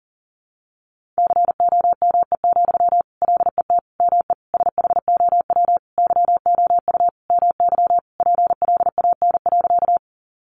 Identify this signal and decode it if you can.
Morse code, heard as a keyed tone, about 40 wpm, 700 Hz. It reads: COME, LET ME SHOW YOU MY PLAN.
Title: Sentences from Top 500 Words 40wpm